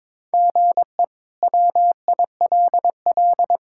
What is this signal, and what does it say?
Morse code, heard as a keyed tone, about 22 wpm, 700 Hz. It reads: GE WILL
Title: K1USN SST QSO Practice - 22wpm